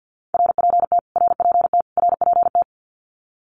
Morse code, heard as a keyed tone, about 50 wpm, 700 Hz. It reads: RPT RPT RPT
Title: Abbreviations Repeated 3x - 50wpm